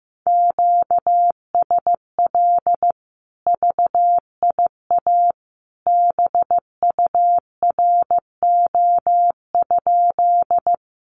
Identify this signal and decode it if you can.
Morse code, heard as a keyed tone, about 15 wpm, 700 Hz. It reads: QSL VIA BURO?
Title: Common QSO Phrases 15wpm